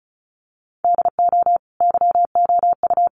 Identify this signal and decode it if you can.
Morse code, heard as a keyed tone, about 34 wpm, 700 Hz. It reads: DO YOU
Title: Sentences from Top 1300 Words 35wpm